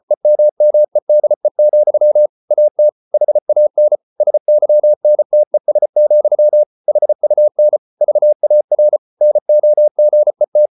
Morse code, heard as a keyed tone, about 34 wpm, 600 Hz. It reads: EMMEDE, AT HAN SYNTES, HUN VAR NOGET
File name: Ferie QRQ 34wpm